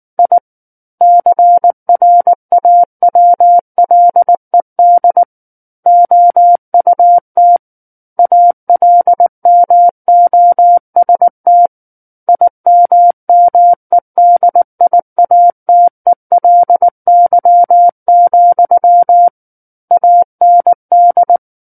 Morse code, heard as a keyed tone, about 19 wpm, 700 Hz. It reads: I CRAWLED OUT ALMOST IMMEDIATELY, AND
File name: Book One / 10-Chapter 10 - 19 WPM